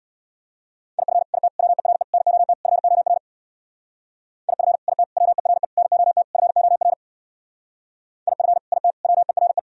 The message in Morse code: IF IT WERE NOT FOR IF IT WERE NOT FOR IF IT WERE

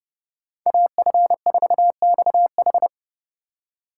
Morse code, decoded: AF4XH

